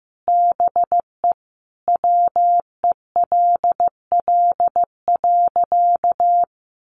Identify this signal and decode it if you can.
Morse code, heard as a keyed tone, about 15 wpm, 700 Hz. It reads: BE WELL.
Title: Sentences from Top 1500 Words 15wpm